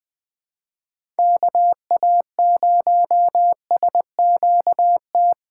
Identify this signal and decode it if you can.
Morse code, heard as a keyed tone, about 20 wpm, 700 Hz. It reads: KA0SQT
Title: USA Call Signs - 6 Characters - 6x Character Spacing - 20wpm